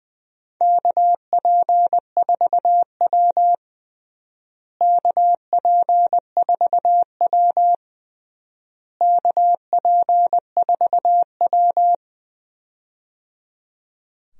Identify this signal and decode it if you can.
Morse code, heard as a keyed tone, about 21 wpm, 700 Hz. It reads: KP4W KP4W KP4W
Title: DXCC-100 Call Signs - 4 Characters - Repeated 3x - 20wpm